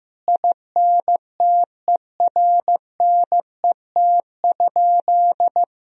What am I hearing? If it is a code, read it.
Morse code, heard as a keyed tone, about 16 wpm, 700 Hz. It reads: INTERNET?
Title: Sentences from Top 1000 Words - Review Part 4 - 15wpm